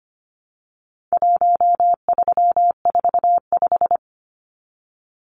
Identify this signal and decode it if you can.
Morse code, heard as a keyed tone, about 25 wpm, 700 Hz. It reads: 1345
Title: Numbers - 4-Digit 25wpm